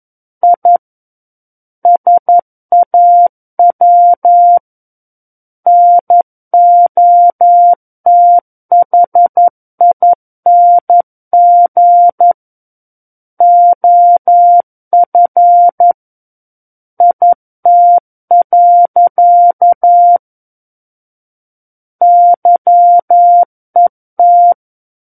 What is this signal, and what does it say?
Morse code, heard as a keyed tone, about 11 wpm, 700 Hz. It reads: I SAW NOTHING OF IT. YET
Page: War of the Worlds - 02-Chapter 2 - 11 WPM